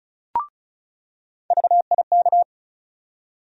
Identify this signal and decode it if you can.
Morse code, heard as a keyed tone, about 34 wpm, 700 Hz. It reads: VIK
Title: USA Call Sign Postfixes - 3 Letters 35wpm